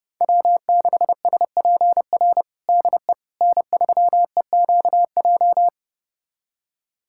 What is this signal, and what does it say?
Morse code, heard as a keyed tone, about 30 wpm, 700 Hz. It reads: W6SPR DE N3EQJ
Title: USA Callsign DE Callsign - 5 Characters - 30wpm